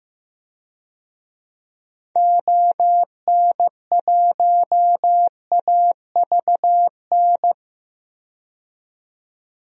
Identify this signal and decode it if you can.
Morse code, heard as a keyed tone, about 15 wpm, 700 Hz. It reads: ON1AVN
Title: DXCC-100 Call Signs - 6 Characters - 15wpm